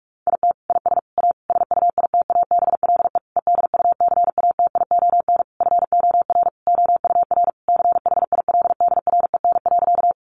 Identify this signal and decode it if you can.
Morse code, heard as a keyed tone, about 44 wpm, 700 Hz. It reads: IT IS A SUITABLE EDUCATION FOR OUR CHILDREN.
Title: Sentences from Top 1400 Words 45wpm